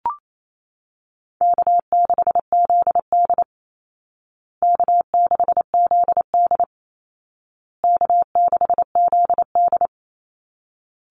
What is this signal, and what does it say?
Morse code, heard as a keyed tone, about 28 wpm, 700 Hz. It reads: K6ZD K6ZD K6ZD
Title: Easy Call Signs Repeated 3x - 28wpm